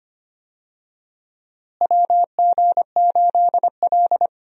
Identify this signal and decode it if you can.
Morse code, heard as a keyed tone, about 25 wpm, 700 Hz. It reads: WG8L
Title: USA Call Signs - 4 Characters - Repeated 3x - 25wpm